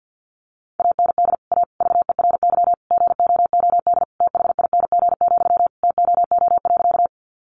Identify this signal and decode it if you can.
Morse code, heard as a keyed tone, about 50 wpm, 700 Hz. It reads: AND A VERY GOOD THING, TOO.
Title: Sentences from Top 200 Words 50wpm